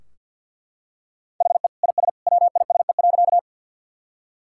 Morse code, heard as a keyed tone, about 51 wpm, 700 Hz. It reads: HE IS WISE.
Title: Sentences from Top 1600 Words 50wpm